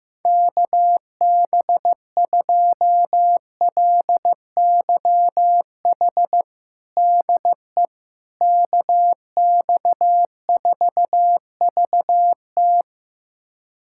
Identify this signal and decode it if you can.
Morse code, heard as a keyed tone, about 16 wpm, 700 Hz. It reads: KB2LYH DE KX4VT
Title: USA Callsign DE Callsign 15wpm